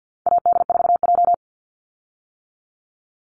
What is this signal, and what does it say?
Morse code, heard as a keyed tone, about 50 wpm, 700 Hz. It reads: AD4J
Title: Call Signs - 50wpm